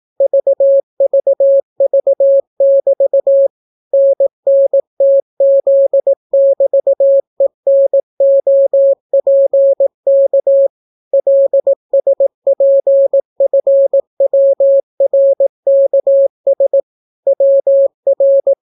Morse code, heard as a keyed tone, about 18 wpm, 550 Hz. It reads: VVV<BT> NNTZ<BT>ENOPK LSPFWRKS WR